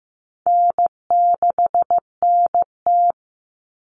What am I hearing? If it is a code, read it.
Morse code, heard as a keyed tone, about 15 wpm, 700 Hz. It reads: N6NT
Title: Common Cut Numbers - 4-Digit 15wpm